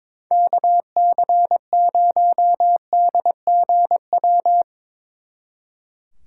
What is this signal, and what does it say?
Morse code, heard as a keyed tone, about 24 wpm, 700 Hz. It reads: KC0DGW
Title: Speed Racing - USA 2x3 Call Signs 22wpm